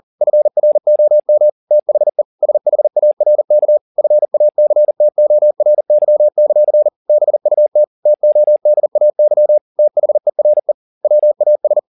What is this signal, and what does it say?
Morse code, heard as a keyed tone, about 40 wpm, 600 Hz. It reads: FROM THE SHARK FACTORY; BUT TODAY THERE WAS
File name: the old man and the sea 1 40wpm